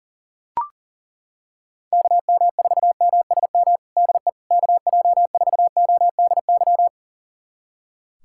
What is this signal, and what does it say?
Morse code, heard as a keyed tone, about 40 wpm, 700 Hz. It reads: KM4MSM DE KJ4ODY
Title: USA Callsign DE Callsign 40wpm